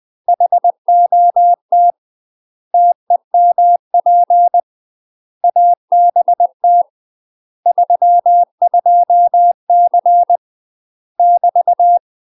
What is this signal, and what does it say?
Morse code, heard as a keyed tone, about 20 wpm, 700 Hz. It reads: HOT TEMP ABT 32C <BT>
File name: UK MASTER QSOs TRAINING / 20WPM / UK MASTER QSOs 0011